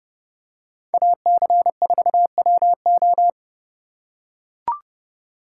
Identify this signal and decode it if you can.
Morse code, heard as a keyed tone, about 29 wpm, 700 Hz. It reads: AC4WO